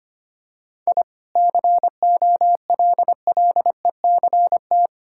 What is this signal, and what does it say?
Morse code, heard as a keyed tone, about 25 wpm, 690 Hz. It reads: I COLLECT
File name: Easy Sentences with Hints - 25wpm